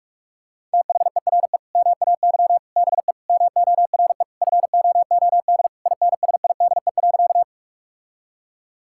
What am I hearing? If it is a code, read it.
Morse code, heard as a keyed tone, about 46 wpm, 700 Hz. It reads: THERE MAY BE MORE FOOD INSIDE.
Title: Sentences from Top 1000 Words - Review Part 6 - 45wpm